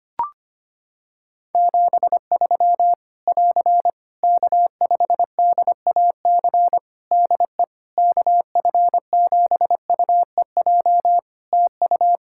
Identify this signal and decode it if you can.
Morse code, heard as a keyed tone, about 25 wpm, 700 Hz. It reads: 73 <AR> K5DAC DE KF7UEJ TU
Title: Practice QSO - Exchange 4 - Ending - 25wpm